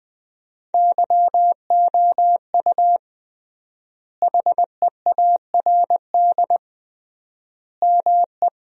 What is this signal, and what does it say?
Morse code, heard as a keyed tone, about 20 wpm, 700 Hz. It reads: YOU HEARD ME
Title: 3x Word-Spacing - Sentences from Top 500 Words - 20wpm